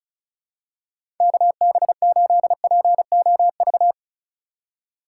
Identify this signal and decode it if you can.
Morse code, heard as a keyed tone, about 36 wpm, 700 Hz. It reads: KB8POV
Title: USA Call Signs - 6 Characters - 2x Character Spacing - 35wpm